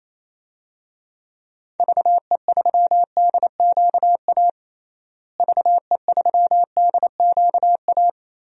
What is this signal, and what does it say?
Morse code, heard as a keyed tone, about 28 wpm, 700 Hz. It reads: VE3DQA VE3DQA
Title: DXCC-100 Call Signs - 6 Characters - Repeated 3x - 28wpm